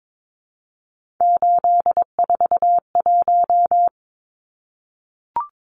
Morse code, heard as a keyed tone, about 22 wpm, 700 Hz. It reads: 841
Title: Numbers - 3-Digit 22wpm